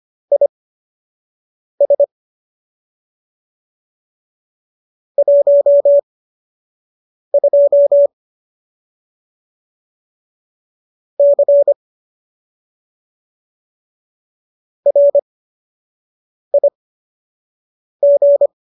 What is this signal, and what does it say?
Morse code, heard as a keyed tone, about 25 wpm, spaced as for 6 wpm, 570 Hz. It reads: IS 12 C RIG
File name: qso003 06